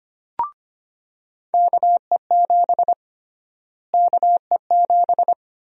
Morse code, heard as a keyed tone, about 25 wpm, 700 Hz. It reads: KE7 KE7
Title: USA Call Sign Prefixes - Repeated 2x - 25wpm